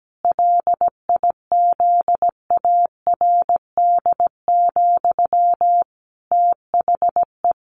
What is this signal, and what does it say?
Morse code, heard as a keyed tone, about 17 wpm, 700 Hz. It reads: LIZARD, THE
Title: A Sound Of Thunder 17wpm